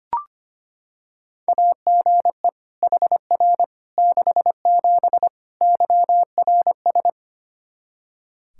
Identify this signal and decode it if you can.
Morse code, heard as a keyed tone, about 25 wpm, 700 Hz. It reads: AGE HR 67 YRS
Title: Practice QSO - Exchange 3 - More Info - 25wpm